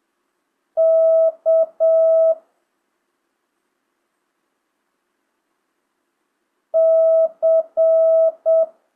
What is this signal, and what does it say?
Morse code, heard as a keyed tone, about 7 wpm, 650 Hz. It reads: K C
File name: jh8cbh / 2025cw